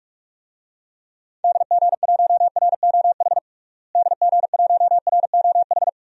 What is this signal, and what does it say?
Morse code, heard as a keyed tone, about 46 wpm, 700 Hz. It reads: DG1ROH DG1ROH
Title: DXCC-100 Call Signs - 6 Characters - Repeated 2x - 45wpm